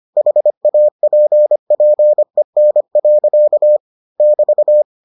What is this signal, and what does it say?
Morse code, heard as a keyed tone, about 25 wpm, 600 Hz. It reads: HAPPEN. <BT>